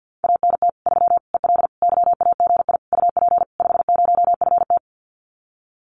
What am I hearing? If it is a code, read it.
Morse code, heard as a keyed tone, about 50 wpm, 700 Hz. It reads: ANT 3 EL YAGI UP 50FT
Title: Common QSO Phrases Encore 50wpm